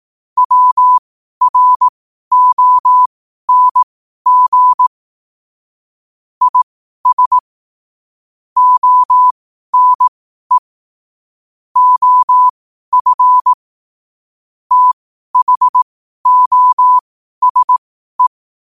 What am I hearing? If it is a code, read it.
Morse code, heard as a keyed tone, about 18 wpm, spaced as for 12 wpm, 1000 Hz. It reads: WRONG IS ONE OF THOSE
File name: qotdfiles / 12wpm